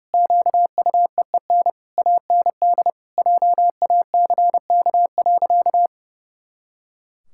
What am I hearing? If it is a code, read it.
Morse code, heard as a keyed tone, about 28 wpm, 700 Hz. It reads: QUEEN AND JACK.
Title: Pangram Challenge v2 - 30wpm